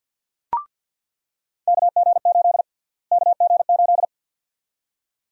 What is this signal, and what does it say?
Morse code, heard as a keyed tone, about 47 wpm, 700 Hz. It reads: KG8 KG8
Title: USA Call Sign Prefixes - Repeated 2x - 50wpm